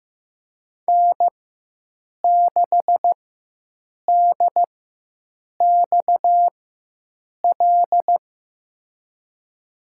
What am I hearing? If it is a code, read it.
Morse code, heard as a keyed tone, about 15 wpm, spaced as for 7 wpm, 700 Hz. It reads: N6DXL
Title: USA Call Signs - 5 Characters - 4x Character Spacing - 15wpm